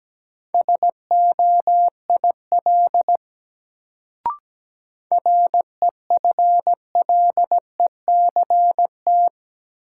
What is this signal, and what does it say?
Morse code, heard as a keyed tone, about 17 wpm, 700 Hz. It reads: SOIL REFLECT
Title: Sentences from Top 1500 Words - New Word Drill Alt - 17wpm